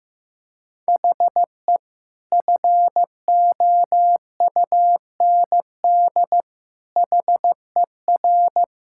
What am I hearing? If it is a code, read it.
Morse code, heard as a keyed tone, about 15 wpm, 700 Hz. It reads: HE FOUND HER
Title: Sentences from Top 1600 Words 15wpm